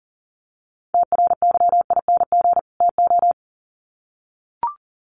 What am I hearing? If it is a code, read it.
Morse code, heard as a keyed tone, about 38 wpm, 700 Hz. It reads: TRYING TO
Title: Sets of 2 Words 40wpm